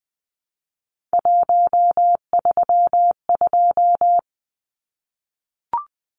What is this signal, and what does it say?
Morse code, heard as a keyed tone, about 20 wpm, 700 Hz. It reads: 132
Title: Numbers - 3-Digit 20wpm